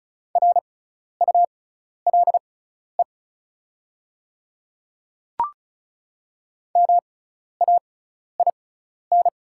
Morse code, heard as a keyed tone, about 36 wpm, spaced as for 12 wpm, 690 Hz. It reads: RULE MAIN
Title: 6x Character Spacing - Top 500 Words - 35wpm